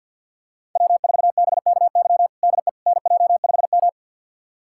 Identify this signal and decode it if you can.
Morse code, heard as a keyed tone, about 50 wpm, 700 Hz. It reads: W4BKY DE NJ5M
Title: USA Callsign DE Callsign 50wpm